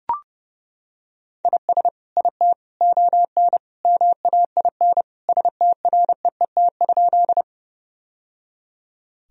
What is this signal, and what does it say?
Morse code, heard as a keyed tone, about 30 wpm, 700 Hz. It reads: IS IT ON MAIN STREET?
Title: Sentences from Top 1000 Words - Review Part 2 - 30wpm